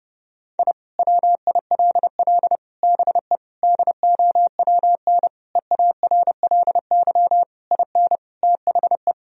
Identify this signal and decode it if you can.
Morse code, heard as a keyed tone, about 31 wpm, 700 Hz. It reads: I WILL BE DOWN EARLY IN THE
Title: Sentences from Top 1000 Words - Review Part 2 - 30wpm